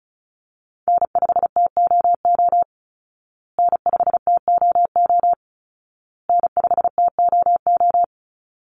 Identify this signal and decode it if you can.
Morse code, heard as a keyed tone, about 34 wpm, 700 Hz. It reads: N5TOO N5TOO N5TOO
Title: Call Signs Repeated 2x - 35wpm